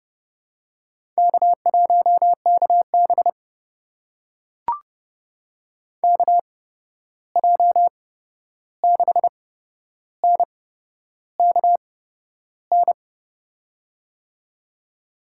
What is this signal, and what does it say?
Morse code, heard as a keyed tone, about 30 wpm, spaced as for 11 wpm, 700 Hz. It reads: K1KB KJ6NKN
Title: USA Call Signs - 8x Character Spacing - 30wpm